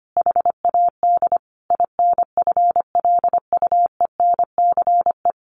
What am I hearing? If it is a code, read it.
Morse code, heard as a keyed tone, about 25 wpm, 700 Hz. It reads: HAD INFLUENCE
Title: Sentences from Top 1100 Words 25wpm